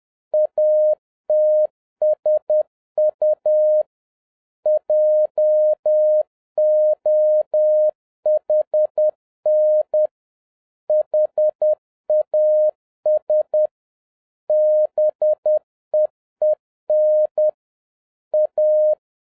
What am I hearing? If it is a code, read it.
Morse code, heard as a keyed tone, about 10 wpm, 610 Hz. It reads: ATSU JOHN HAS BEEN A
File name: 2011-07-mp3 / n46 JOHN K3TN 10wpm